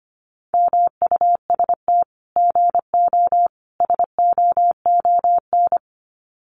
Morse code, heard as a keyed tone, about 25 wpm, 700 Hz. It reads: MUST GO SOON
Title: Common QSO Phrases 25wpm